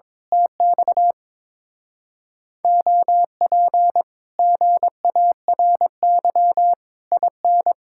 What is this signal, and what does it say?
Morse code, heard as a keyed tone, about 22 wpm, 700 Hz. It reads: TX OP GARY IN
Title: Speed-Racing Common QSO Phrases Encore - 22wpm